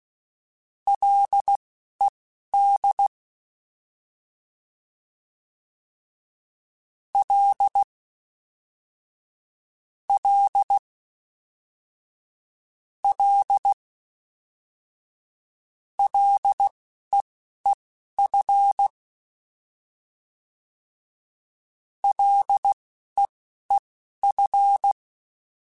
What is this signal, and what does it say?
Morse code, heard as a keyed tone, about 16 wpm, spaced as for 12 wpm, 800 Hz. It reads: LED L L L LEEF LEEF
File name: Ls / Single L m3s / Single L Prefixe 3x2 Twice @10